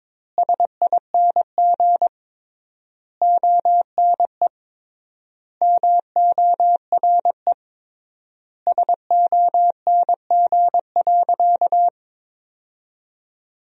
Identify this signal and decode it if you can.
Morse code, heard as a keyed tone, about 22 wpm, 700 Hz. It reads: SING ONE MORE SONG.
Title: 3x Word-Spacing - Sentences from Top 400 Words - 22wpm